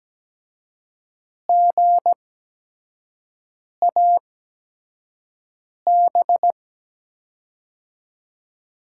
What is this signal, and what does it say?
Morse code, heard as a keyed tone, about 17 wpm, spaced as for 5 wpm, 700 Hz. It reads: GAB
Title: USA Call Signs - 6 Characters - 8x Character Spacing - 17wpm